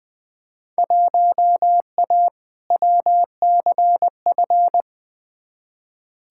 Morse code, heard as a keyed tone, about 21 wpm, 700 Hz. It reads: 1A WCF
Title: Field Day Exchanges - 20wpm